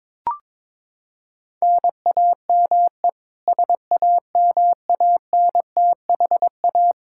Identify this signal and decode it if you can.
Morse code, heard as a keyed tone, about 22 wpm, 700 Hz. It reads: NAME SAMANTHA
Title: Practice QSO - Exchange 1 - Beginning - 22wpm